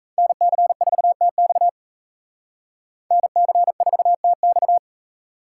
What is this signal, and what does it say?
Morse code, heard as a keyed tone, about 36 wpm, 700 Hz. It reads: NC4TX NC4TX